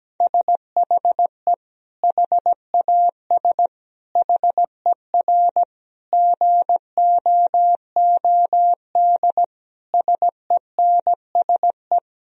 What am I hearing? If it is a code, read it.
Morse code, heard as a keyed tone, about 17 wpm, 700 Hz. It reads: SHE HAS HER GOOD SENSE
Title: Sentences from Top 1000 Words - Review Part 4 - 17wpm